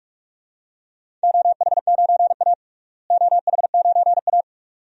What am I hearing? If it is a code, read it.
Morse code, heard as a keyed tone, about 46 wpm, 700 Hz. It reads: OH9A OH9A
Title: DXCC-100 Call Signs - 4 Characters - Repeated 2x - 45wpm